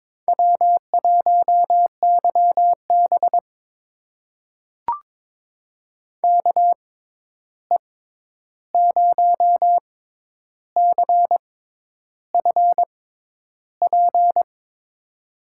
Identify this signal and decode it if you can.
Morse code, heard as a keyed tone, about 22 wpm, spaced as for 10 wpm, 700 Hz. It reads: W1YB KE0CFP
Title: USA Call Signs - 6x Character Spacing - 22wpm